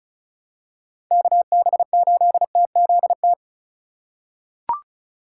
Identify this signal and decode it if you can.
Morse code, heard as a keyed tone, about 34 wpm, 700 Hz. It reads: KB8TZT
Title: USA Call Signs - 6 Characters - 4x Character Spacing - 35wpm